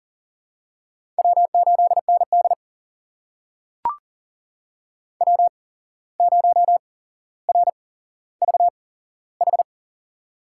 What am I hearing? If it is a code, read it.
Morse code, heard as a keyed tone, about 42 wpm, spaced as for 16 wpm, 700 Hz. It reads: W8ND W0RVH
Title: USA Call Signs - 8x Character Spacing - 40wpm